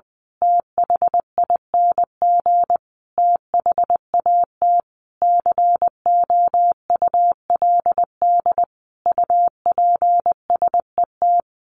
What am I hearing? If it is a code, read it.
Morse code, heard as a keyed tone, about 20 wpm, 700 Hz. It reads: THING THAT COULD UPSET
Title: A Sound Of Thunder 20wpm